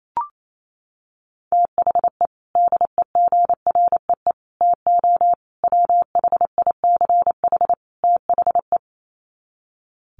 Morse code, heard as a keyed tone, about 28 wpm, 700 Hz. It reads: THE DEGREE TO WHICH THE
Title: Sets of 5 Words Encore2 28wpm